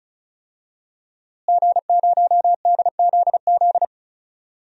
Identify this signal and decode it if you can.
Morse code, heard as a keyed tone, about 36 wpm, 700 Hz. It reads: G0DZZ
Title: DXCC-100 Call Signs - 5 Characters - Repeated 3x - 35wpm